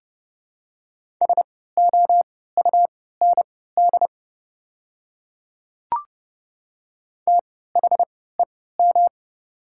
Morse code, heard as a keyed tone, about 30 wpm, spaced as for 17 wpm, 700 Hz. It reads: SOUND THEM
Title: 3x Character Spacing - Top 100 Words - 30wpm